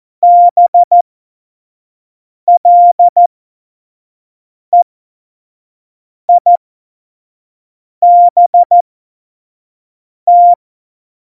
Morse code, heard as a keyed tone, about 14 wpm, spaced as for 5 wpm, 700 Hz. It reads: BLEIBT
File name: Morsekurs / Witze / Witz 24 April 2018, 25 F70